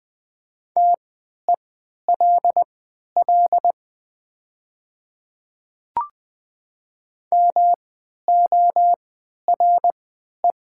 Morse code, heard as a keyed tone, about 20 wpm, spaced as for 11 wpm, 700 Hz. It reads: TELL MORE
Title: 3x Character Spacing - Top 300 Words - 20wpm